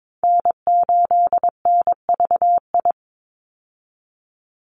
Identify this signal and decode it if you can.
Morse code, heard as a keyed tone, about 22 wpm, 700 Hz. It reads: N8NVI
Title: USA Call Signs 22wpm